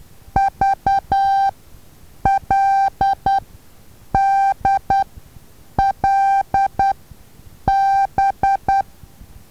Morse code, heard as a keyed tone, about 10 wpm, spaced as for 7 wpm, 780 Hz. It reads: VLDLB
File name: Taviro HA6VA B